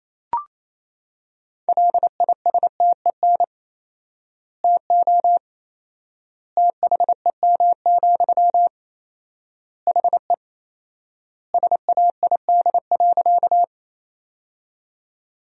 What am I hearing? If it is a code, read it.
Morse code, heard as a keyed tone, about 27 wpm, 700 Hz. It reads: LISTEN TO THEM, HE SAID.
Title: 4x Word-Spacing - Sentences from Top 400 Words - 28wpm